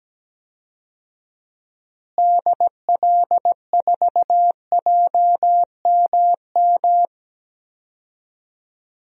Morse code, heard as a keyed tone, about 17 wpm, 700 Hz. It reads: DL4JMM
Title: DXCC-100 Call Signs - 6 Characters - Repeated 2x - 17wpm